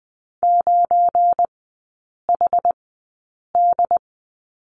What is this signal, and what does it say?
Morse code, heard as a keyed tone, about 20 wpm, spaced as for 8 wpm, 690 Hz. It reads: 9HD
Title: 172 Lesson 28e - Number 9 - Three Characters 20wpm